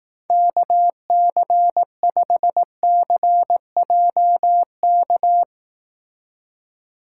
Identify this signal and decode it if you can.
Morse code, heard as a keyed tone, about 18 wpm, 700 Hz. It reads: KC5CJK
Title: Speed Racing - USA 2x3 Call Signs 15wpm